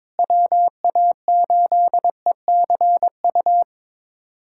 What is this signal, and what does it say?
Morse code, heard as a keyed tone, about 22 wpm, 700 Hz. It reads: WA8ECU
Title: USA Callsign DE Callsign - 6 Characters - 22wpm